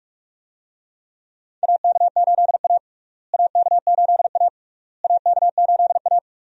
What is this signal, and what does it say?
Morse code, heard as a keyed tone, about 43 wpm, 700 Hz. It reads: AK8A AK8A AK8A
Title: USA Call Signs - 4 Characters - Repeated 3x - 45wpm